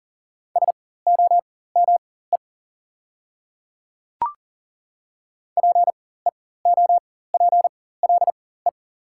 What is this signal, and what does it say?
Morse code, heard as a keyed tone, about 39 wpm, spaced as for 19 wpm, 700 Hz. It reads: SOME PEOPLE
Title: 4x Character Spacing - Top 200 Words - 40wpm